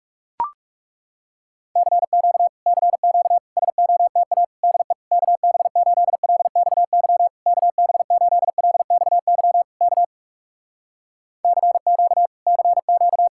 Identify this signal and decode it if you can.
Morse code, heard as a keyed tone, about 40 wpm, 700 Hz. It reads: CQ CQ SOTA DE KB8LXY KB8LXY K CQ CQ